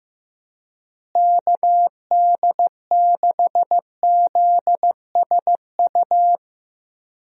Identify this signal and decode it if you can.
Morse code, heard as a keyed tone, about 15 wpm, 700 Hz. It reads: KD6ZSU
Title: USA Call Signs - 6 Characters - Repeated 2x - 15wpm